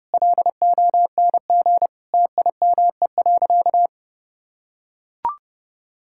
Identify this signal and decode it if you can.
Morse code, heard as a keyed tone, about 30 wpm, 700 Hz. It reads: LONG TIME.
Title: Sentences from Top 1000 Words - Review Part 6 - 30wpm